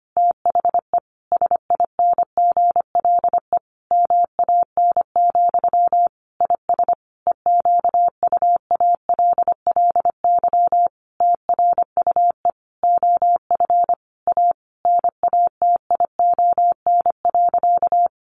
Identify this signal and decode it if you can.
Morse code, heard as a keyed tone, about 25 wpm, 700 Hz. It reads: THE SINGLE MAN, IS EQUALLY TRUE OF A NATION.
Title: Sentences from Top 1400 Words 25wpm